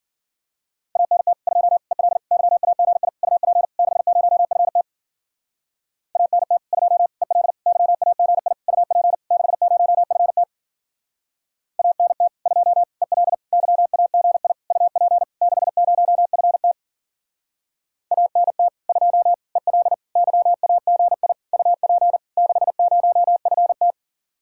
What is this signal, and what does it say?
Morse code, heard as a keyed tone, about 49 wpm, 700 Hz. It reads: ANT 2 EL YAGI UP 60FT ANT 2 EL YAGI UP 60FT ANT 2 EL YAGI UP 60FT ANT 2 EL YAGI UP 60FT
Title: Speed-Racing Common QSO Phrases Encore - 40wpm